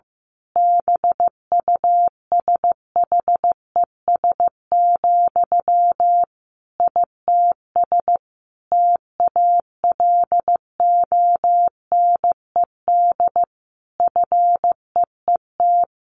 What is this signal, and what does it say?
Morse code, heard as a keyed tone, about 15 wpm, 700 Hz. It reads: BUSHES, ITS TALONED FEET
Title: A Sound Of Thunder 15wpm